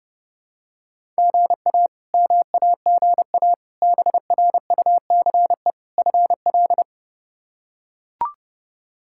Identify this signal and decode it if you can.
Morse code, heard as a keyed tone, about 30 wpm, 700 Hz. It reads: GA MAGA BRUCE FL
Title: K1USN SST QSO Practice - 30wpm